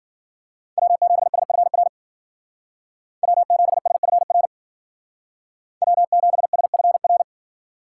Morse code, heard as a keyed tone, about 49 wpm, 700 Hz. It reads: W7SFR W7SFR W7SFR
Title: Speed Racing - USA 1x3 Call Signs 40wpm